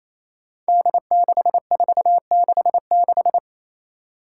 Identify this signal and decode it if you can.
Morse code, heard as a keyed tone, about 27 wpm, 700 Hz. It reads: D6466
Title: Five Challenging Letter-Number Combinations 28wpm